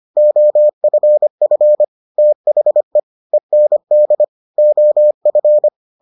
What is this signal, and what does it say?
Morse code, heard as a keyed tone, about 25 wpm, 600 Hz. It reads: OFF THE END OF